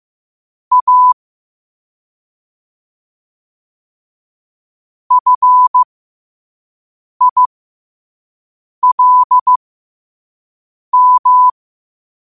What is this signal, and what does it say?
Morse code, heard as a keyed tone, about 15 wpm, spaced as for 5 wpm, 1000 Hz. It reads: A FILM
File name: qotdfiles / 5wpm